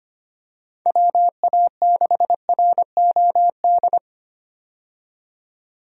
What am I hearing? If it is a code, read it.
Morse code, heard as a keyed tone, about 25 wpm, 700 Hz. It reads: WA6ROD
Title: USA Call Signs - 6 Characters - 25wpm